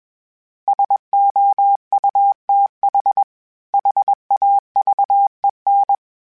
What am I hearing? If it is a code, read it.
Morse code, heard as a keyed tone, about 21 wpm, 800 Hz. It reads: SOUTH HAVEN
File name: fromSMs / FromSMs mp3s / Cities@20